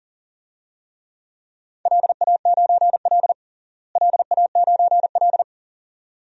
Morse code, heard as a keyed tone, about 40 wpm, 700 Hz. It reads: LA9L LA9L
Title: DXCC-100 Call Signs - 4 Characters - Repeated 2x - 40wpm